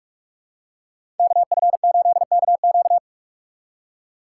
Morse code, heard as a keyed tone, about 44 wpm, 700 Hz. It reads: KF8KQ
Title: USA Call Signs - 5 Characters - 6x Character Spacing - 45wpm